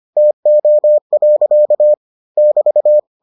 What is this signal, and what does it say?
Morse code, heard as a keyed tone, about 25 wpm, 600 Hz. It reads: TO. <BT>